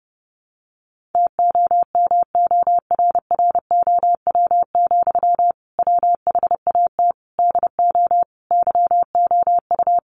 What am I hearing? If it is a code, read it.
Morse code, heard as a keyed tone, about 30 wpm, 700 Hz. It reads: TOMORROW, WHAT DO YOU
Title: Sentences from Top 1500 Words 30wpm